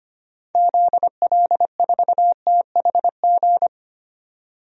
Morse code, heard as a keyed tone, about 25 wpm, 700 Hz. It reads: ZL4THG
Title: DXCC-100 Call Signs - 6 Characters - 25wpm